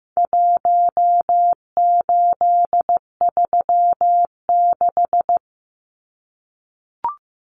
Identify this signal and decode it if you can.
Morse code, heard as a keyed tone, about 15 wpm, 700 Hz. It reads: 1836
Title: Numbers - Contest Style - 4-Digit 15wpm